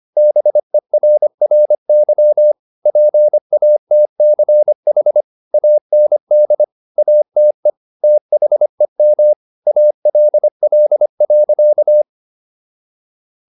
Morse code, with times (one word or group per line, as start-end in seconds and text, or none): 0.2-2.5 BERRY
2.9-5.2 PATCH
5.5-6.6 AND
7.0-7.7 ATE
8.0-9.3 THEM
9.7-12.0 ALL.